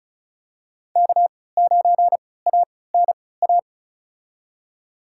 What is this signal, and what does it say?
Morse code, heard as a keyed tone, about 36 wpm, spaced as for 20 wpm, 700 Hz. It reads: K9ANA
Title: USA Call Signs - 5 Characters - 3x Character Spacing - 35wpm